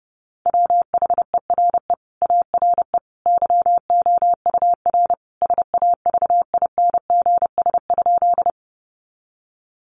Morse code, heard as a keyed tone, about 30 wpm, 700 Hz. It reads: WHERE ARE YOUR SAVINGS?
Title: Sentences from Top 1200 Words 30wpm